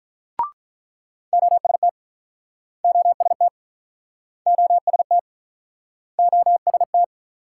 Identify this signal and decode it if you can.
Morse code, heard as a keyed tone, about 46 wpm, 700 Hz. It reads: OST OST OST OST